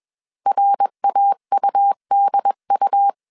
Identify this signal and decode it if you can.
Morse code, heard as a keyed tone, about 21 wpm, 800 Hz. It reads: RAUDU